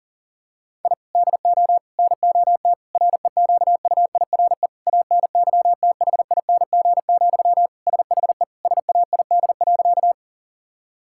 Morse code, heard as a keyed tone, about 40 wpm, 700 Hz. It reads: I DO NOT REQUIRE ANYTHING, SHE SAID.
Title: Sentences from Top 1000 Words - Review Part 6 - 40wpm